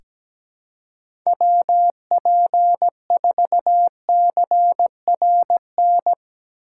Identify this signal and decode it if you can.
Morse code, heard as a keyed tone, about 17 wpm, 700 Hz. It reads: WP4CRN
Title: USA Call Signs - 6 Characters - 4x Character Spacing - 17wpm